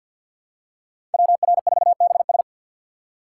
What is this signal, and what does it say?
Morse code, heard as a keyed tone, about 51 wpm, 700 Hz. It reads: WR4BS
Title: USA Call Signs - 5 Characters - 2x Character Spacing - 50wpm